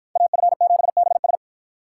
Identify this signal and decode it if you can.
Morse code, heard as a keyed tone, about 53 wpm, 700 Hz. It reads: AF7BS